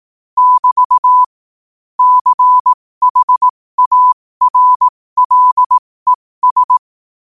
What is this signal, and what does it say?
Morse code, heard as a keyed tone, about 18 wpm, spaced as for 15 wpm, 1000 Hz. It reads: <BT> CHARLES
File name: qotdfiles / 15wpm